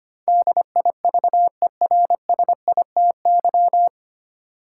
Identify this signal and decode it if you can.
Morse code, heard as a keyed tone, about 25 wpm, 700 Hz. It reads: DIVERSITY
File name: practice / 25wpm / 9 Letter Words - First 3 Letters Given - 25wpm